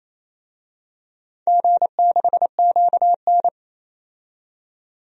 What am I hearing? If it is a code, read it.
Morse code, heard as a keyed tone, about 28 wpm, 700 Hz. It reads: G6QN
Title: DXCC-100 Call Signs - 4 Characters - 28wpm